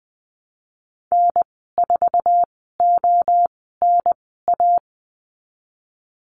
Morse code, heard as a keyed tone, about 20 wpm, spaced as for 15 wpm, 700 Hz. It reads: N4ONA
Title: 128 Lesson 21g - Number 6 - Call Signs 20wpm